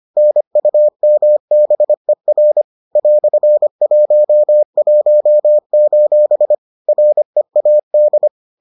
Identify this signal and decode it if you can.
Morse code, heard as a keyed tone, about 25 wpm, 600 Hz. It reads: NUMBER "11: READ